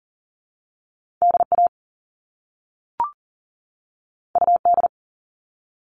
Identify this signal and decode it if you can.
Morse code, heard as a keyed tone, about 39 wpm, 700 Hz. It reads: DA UD
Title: All Cut Numbers - 2-Digit 40wpm